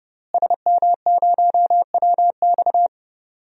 Morse code, heard as a keyed tone, about 32 wpm, 700 Hz. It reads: SM0WX
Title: DXCC-100 Call Signs - 5 Characters - Repeated 2x - 30wpm